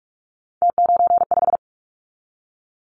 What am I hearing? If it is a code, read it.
Morse code, heard as a keyed tone, about 46 wpm, 700 Hz. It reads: T95
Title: Callsign Prefixes for Top 100 DXCC - 45wpm